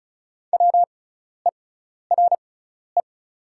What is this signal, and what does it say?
Morse code, heard as a keyed tone, about 34 wpm, spaced as for 12 wpm, 700 Hz. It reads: WERE
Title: 6x Character Spacing - Top 300 Words - 35wpm